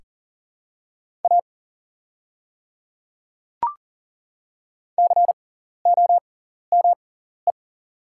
Morse code, heard as a keyed tone, about 38 wpm, spaced as for 14 wpm, 700 Hz. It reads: A COME
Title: 6x Character Spacing - Top 100 Words - 40wpm